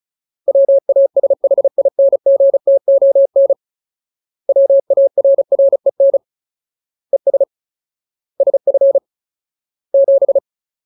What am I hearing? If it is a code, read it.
Morse code, heard as a keyed tone, about 34 wpm, 550 Hz. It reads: WASHINGTON WARREN ES SF 7
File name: Ultimate Word List / 35WPM Ultimate